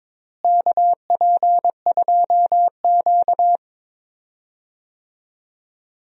Morse code, heard as a keyed tone, about 22 wpm, 700 Hz. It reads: KP2Q
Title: DXCC-100 Call Signs - 4 Characters - Repeated 2x - 22wpm